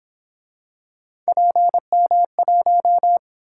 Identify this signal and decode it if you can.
Morse code, heard as a keyed tone, about 27 wpm, 700 Hz. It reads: PM1
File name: Speed-Racing - Three Letter-Number Combinations - 22wpm